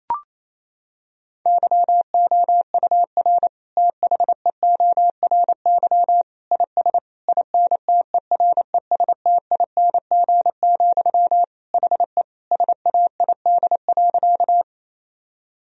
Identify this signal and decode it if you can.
Morse code, heard as a keyed tone, about 28 wpm, 700 Hz. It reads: YOUR THEORY IS INTERESTING, HE SAID.
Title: Sentences from Top 1000 Words - Review Part 3 - 28wpm